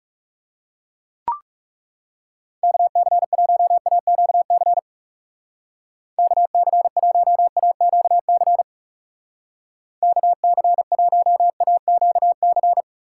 Morse code, heard as a keyed tone, about 36 wpm, 700 Hz. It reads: KC1AQC KC1AQC KC1AQC
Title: Speed Racing - USA Call Signs 30wpm